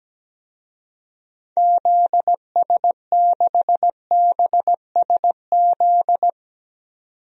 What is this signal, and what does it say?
Morse code, heard as a keyed tone, about 17 wpm, 700 Hz. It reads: ZS6BSZ
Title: DXCC-100 Call Signs - 6 Characters - 17wpm